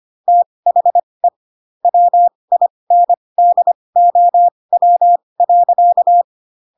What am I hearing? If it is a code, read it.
Morse code, heard as a keyed tone, about 25 wpm, spaced as for 20 wpm, 700 Hz. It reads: THE WINDOW.
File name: Pinocchio / Pinocchio ch5 15 25 ik1hge